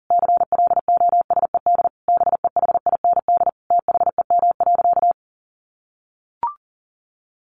Morse code, heard as a keyed tone, about 40 wpm, 700 Hz. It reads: CLOSED BEHIND THEM.
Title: Sentences from Top 1400 Words 40wpm